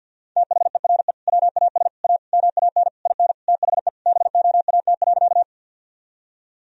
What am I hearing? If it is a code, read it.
Morse code, heard as a keyed tone, about 46 wpm, 700 Hz. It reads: THERE WAS A MAN IN THE BOAT.
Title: Sentences from Top 1000 Words - Review Part 3 - 50wpm